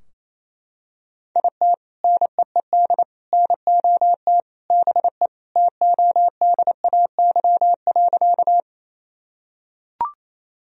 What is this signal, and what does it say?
Morse code, heard as a keyed tone, about 27 wpm, 700 Hz. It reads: IT NEED NOT BE TODAY.
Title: Sentences from Top 1000 Words - Review Part 4 - 28wpm